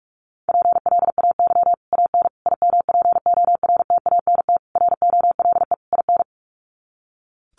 Morse code, heard as a keyed tone, about 44 wpm, 700 Hz. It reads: PLAY AN IMPORTANT ROLE IN
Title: Sets of 5 Words Encore 45wpm